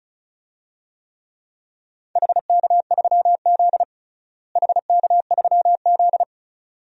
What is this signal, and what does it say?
Morse code, heard as a keyed tone, about 34 wpm, 700 Hz. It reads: HK3Z HK3Z
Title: DXCC-100 Call Signs - 4 Characters - Repeated 2x - 35wpm